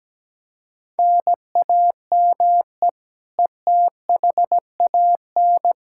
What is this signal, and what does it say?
Morse code, heard as a keyed tone, about 17 wpm, 700 Hz. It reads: NAME ETHAN
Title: Practice QSO - Exchange 1 - Beginning - 17wpm